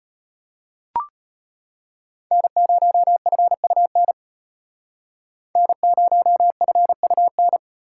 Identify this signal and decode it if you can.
Morse code, heard as a keyed tone, about 33 wpm, 700 Hz. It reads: N0FUN N0FUN